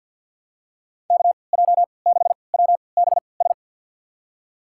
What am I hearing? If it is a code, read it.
Morse code, heard as a keyed tone, about 49 wpm, spaced as for 28 wpm, 700 Hz. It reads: KJ6WBS
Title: USA Call Signs - 3x Character Spacing - 50wpm